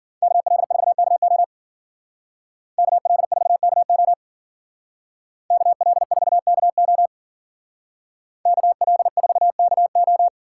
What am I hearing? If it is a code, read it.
Morse code, heard as a keyed tone, about 52 wpm, 700 Hz. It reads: KL4KO KL4KO KL4KO KL4KO